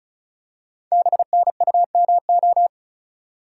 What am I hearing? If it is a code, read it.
Morse code, heard as a keyed tone, about 36 wpm, 690 Hz. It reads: BNUMO
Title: Code Groups - 5 Letters - 35wpm